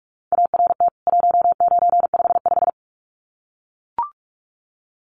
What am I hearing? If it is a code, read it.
Morse code, heard as a keyed tone, about 44 wpm, 700 Hz. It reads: ART 1955
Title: CWT Exchange 45wpm